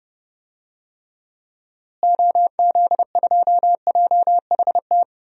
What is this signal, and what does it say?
Morse code, heard as a keyed tone, about 31 wpm, 700 Hz. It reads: OZ2JHT
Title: DXCC-100 Call Signs - 6 Characters - Repeated 2x - 30wpm